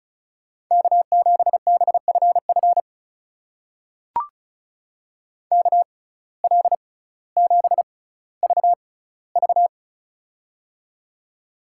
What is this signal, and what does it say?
Morse code, heard as a keyed tone, about 36 wpm, spaced as for 18 wpm, 700 Hz. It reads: K7BFF KL7VV
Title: USA Call Signs - 5 Characters - 6x Character Spacing - 35wpm